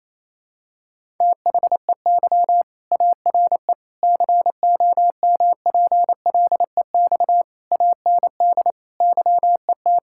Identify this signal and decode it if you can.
Morse code, heard as a keyed tone, about 28 wpm, 700 Hz. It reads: THEY ARE COMPLEX AND YET
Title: Sentences from Top 1000 Words - Review Part 6 - 28wpm